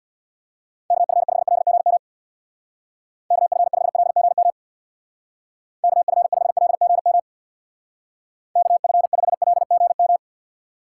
Morse code, heard as a keyed tone, about 59 wpm, 700 Hz. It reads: KF5LGM KF5LGM KF5LGM KF5LGM
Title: Speed Racing - USA 2x3 Call Signs 50wpm